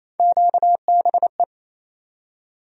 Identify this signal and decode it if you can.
Morse code, heard as a keyed tone, about 27 wpm, 700 Hz. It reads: QBE
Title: USA Call Sign Postfixes - 3 Letters 28wpm